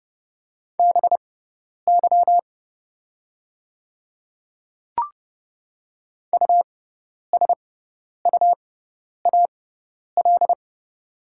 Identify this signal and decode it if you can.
Morse code, heard as a keyed tone, about 31 wpm, spaced as for 10 wpm, 690 Hz. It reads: BY USUAL
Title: 6x Character Spacing - Top 500 Words - 30wpm